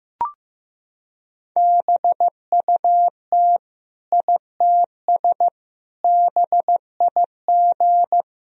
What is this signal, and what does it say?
Morse code, heard as a keyed tone, about 15 wpm, 700 Hz. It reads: BUT ITS BIG
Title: Sentences from Top 1000 Words - Review Part 1 - 15wpm